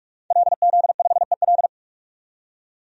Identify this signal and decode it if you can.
Morse code, heard as a keyed tone, about 42 wpm, 700 Hz. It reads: LZ5EL
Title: DXCC-100 Call Signs - 5 Characters - 45wpm